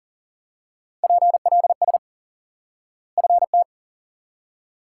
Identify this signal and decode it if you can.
Morse code, heard as a keyed tone, about 40 wpm, 700 Hz. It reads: PLS FT
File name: IQE Course 3 - Lesson 89 - YAGI - Two Elements - 40wpm